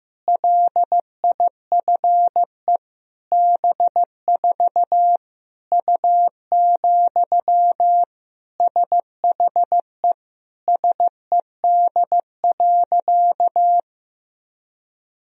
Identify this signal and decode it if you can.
Morse code, heard as a keyed tone, about 15 wpm, 700 Hz. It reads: LIFE B4 U, SHE SED.
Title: Sentences from Top 400 Words and Abbreviations 15wpm